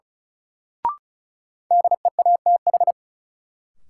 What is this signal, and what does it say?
Morse code, heard as a keyed tone, about 36 wpm, 700 Hz. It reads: DEATH
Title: Sentences from Top 1100 Words - New Word Drill - 35wpm